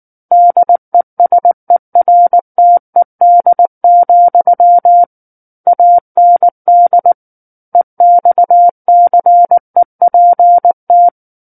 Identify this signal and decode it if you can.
Morse code, heard as a keyed tone, about 19 wpm, 700 Hz. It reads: DESERTED, AND EXCEPT
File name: Book One / 10-Chapter 10 - 19 WPM